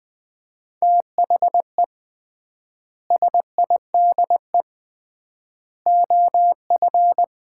Morse code, heard as a keyed tone, about 21 wpm, 700 Hz. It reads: THE SIDE OF
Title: 3x Word-Spacing - Sets of 3 Words - 20wpm